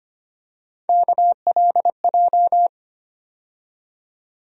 Morse code, heard as a keyed tone, about 25 wpm, 700 Hz. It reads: KLJ
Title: USA Call Sign Postfixes - 3 Letters 25wpm